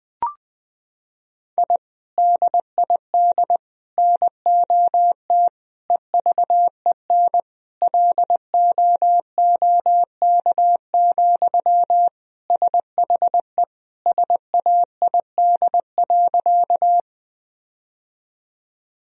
Morse code, heard as a keyed tone, about 21 wpm, 700 Hz. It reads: I DID NOT EVEN LOOK, SHE SAID.
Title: Sentences from Top 1000 Words - Review Part 1 - 20wpm